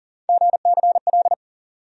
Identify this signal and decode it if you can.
Morse code, heard as a keyed tone, about 40 wpm, 700 Hz. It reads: GCL
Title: USA Call Sign Postfixes - 3 Letters 40wpm